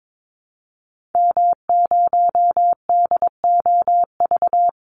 Morse code, heard as a keyed tone, about 22 wpm, 700 Hz. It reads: M0DOV
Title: Call Signs Repeated 2x - 22wpm